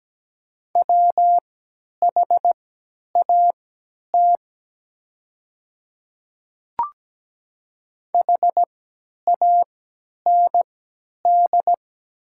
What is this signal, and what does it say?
Morse code, heard as a keyed tone, about 17 wpm, spaced as for 10 wpm, 700 Hz. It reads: WHAT HAND
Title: 3x Character Spacing - Top 200 Words - 17wpm